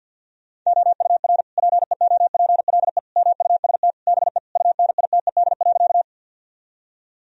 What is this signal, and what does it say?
Morse code, heard as a keyed tone, about 50 wpm, 700 Hz. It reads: OUR PEOPLE MUST BE UNITED.
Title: Sentences from Top 1000 Words - Review Part 4 - 50wpm